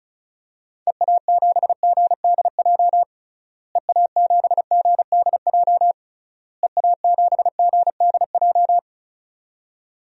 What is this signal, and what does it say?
Morse code, heard as a keyed tone, about 36 wpm, 700 Hz. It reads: EA7GDJ EA7GDJ EA7GDJ
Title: DXCC-100 Call Signs - 6 Characters - Repeated 3x - 35wpm